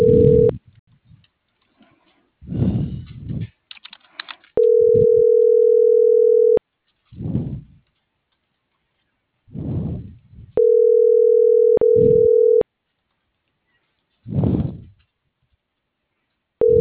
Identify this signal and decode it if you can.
Ringback tone (North America), 440 + 480 Hz.